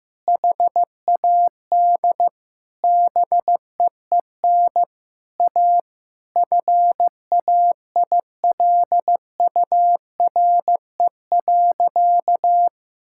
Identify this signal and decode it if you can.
Morse code, heard as a keyed tone, about 14 wpm, 700 Hz. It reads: HAD BEEN A FAILURE.
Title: Sentences from Top 1000 Words - Review Part 6 - 15wpm